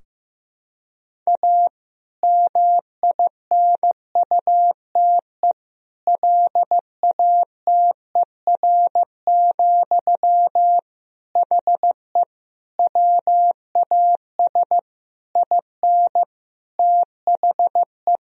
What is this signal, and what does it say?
Morse code, heard as a keyed tone, about 16 wpm, 700 Hz. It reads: A MINUTE LATER, HE WAS IN THE
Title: Sentences from Top 1000 Words - Review Part 4 - 15wpm